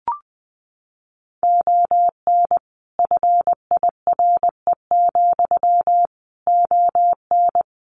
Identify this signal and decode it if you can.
Morse code, heard as a keyed tone, about 19 wpm, 700 Hz. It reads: ON FIRE, ON
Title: Sentences from Top 400 Words 20wpm